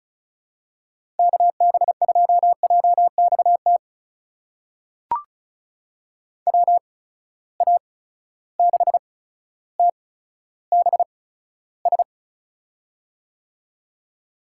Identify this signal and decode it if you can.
Morse code, heard as a keyed tone, about 34 wpm, spaced as for 15 wpm, 700 Hz. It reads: KB2JXT WA6TBS
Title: USA Call Signs - 6 Characters - 8x Character Spacing - 35wpm